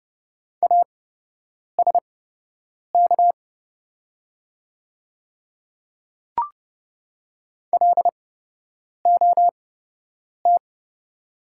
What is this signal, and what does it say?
Morse code, heard as a keyed tone, about 30 wpm, spaced as for 8 wpm, 690 Hz. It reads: ASK LOT
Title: 8x Character Spacing - Top 500 Words - 30wpm